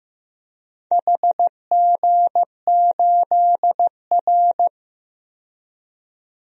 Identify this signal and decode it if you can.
Morse code, heard as a keyed tone, about 15 wpm, 700 Hz. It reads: HG8R
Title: DXCC-100 Call Signs - 4 Characters - 15wpm